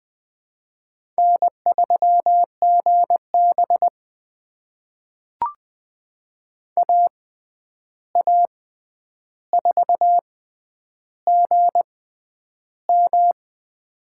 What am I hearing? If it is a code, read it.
Morse code, heard as a keyed tone, about 20 wpm, spaced as for 10 wpm, 700 Hz. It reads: N3GB AA4GM
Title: USA Call Signs - 6x Character Spacing - 20wpm